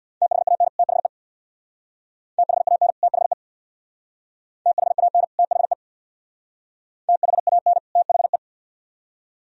Morse code, heard as a keyed tone, about 61 wpm, 700 Hz. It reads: THAN THE THAN THE THAN THE THAN THE
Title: Speed-Racing - Sets of 2 Words 50wpm